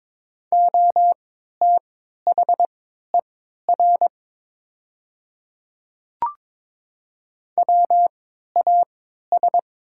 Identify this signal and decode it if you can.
Morse code, heard as a keyed tone, about 22 wpm, spaced as for 13 wpm, 700 Hz. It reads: OTHER WAS
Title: 3x Character Spacing - Top 100 Words - 22wpm